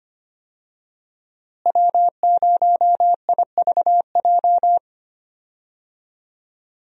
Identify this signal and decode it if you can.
Morse code, heard as a keyed tone, about 25 wpm, 700 Hz. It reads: W0IVJ
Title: USA Call Signs - 5 Characters - Repeated 2x - 25wpm